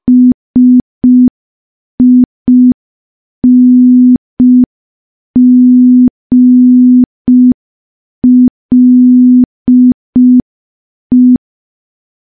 Morse code, heard as a keyed tone, about 5 wpm, 250 Hz. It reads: SINGLE
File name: morse / morse-training-05WPM